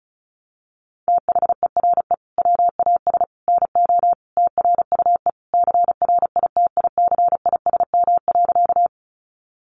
Morse code, heard as a keyed tone, about 34 wpm, 700 Hz. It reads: THERE WAS NO TRUE CRITICISM.
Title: Sentences from Top 1300 Words 35wpm